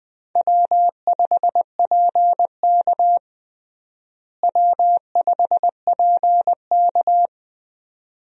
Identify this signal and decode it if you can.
Morse code, heard as a keyed tone, about 20 wpm, 700 Hz. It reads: W5PK W5PK
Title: USA Call Signs - 4 Characters - Repeated 2x - 20wpm